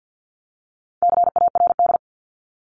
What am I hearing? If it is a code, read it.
Morse code, heard as a keyed tone, about 50 wpm, 700 Hz. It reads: CARD
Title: Top 501-750 Words 50wpm